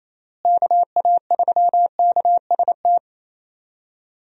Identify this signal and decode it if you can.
Morse code, heard as a keyed tone, about 28 wpm, 700 Hz. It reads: KA3KST
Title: USA Call Signs - 6 Characters - Repeated 3x - 28wpm